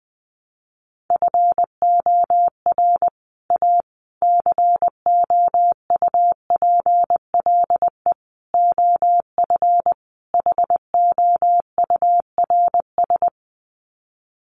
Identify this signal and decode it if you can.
Morse code, heard as a keyed tone, about 20 wpm, 700 Hz. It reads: FOR A COUPLE OF HOURS
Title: Sets of 5 Words Encore2 20wpm